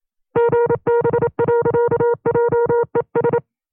Morse code, heard as a keyed tone, about 27 wpm, 470 Hz. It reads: GB.JES
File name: FreeFiles / CW-practice / 25WPM-side1